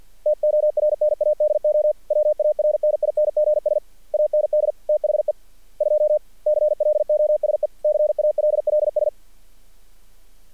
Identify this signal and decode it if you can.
Morse code, heard as a keyed tone, about 48 wpm, 600 Hz. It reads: TORNADO WARNINGS AND THE 2 CLOSE CALLS
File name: archived / CRSnet